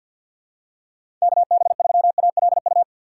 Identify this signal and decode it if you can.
Morse code, heard as a keyed tone, about 52 wpm, 700 Hz. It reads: KB3ALU
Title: USA Call Signs - 6 Characters - 4x Character Spacing - 50wpm